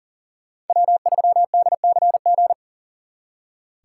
Morse code, heard as a keyed tone, about 51 wpm, 700 Hz. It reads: W3DCG